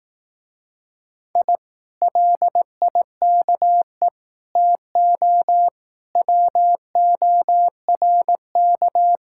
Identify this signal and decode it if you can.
Morse code, heard as a keyed tone, about 18 wpm, 700 Hz. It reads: I LIKE TO WORK
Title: Speed-Racing Common QSO Phrases Encore - 15wpm